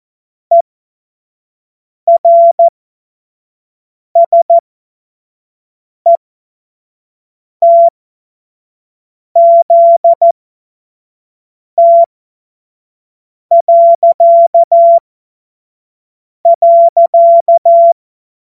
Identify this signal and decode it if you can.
Morse code, heard as a keyed tone, about 14 wpm, spaced as for 5 wpm, 680 Hz. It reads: ERSETZT..
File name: Witz-07-Oktober-2025-25-F70